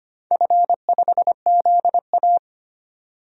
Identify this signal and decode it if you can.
Morse code, heard as a keyed tone, about 25 wpm, 700 Hz. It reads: F5ZA
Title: DXCC-100 Call Signs - 4 Characters - 25wpm